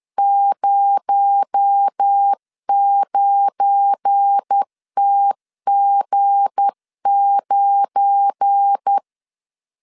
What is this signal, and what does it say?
Morse code, heard as a keyed tone, about 10 wpm, 800 Hz. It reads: 09TG9
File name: LEZIONI / 7 GF9Z / 7 GF9 10wpm 2747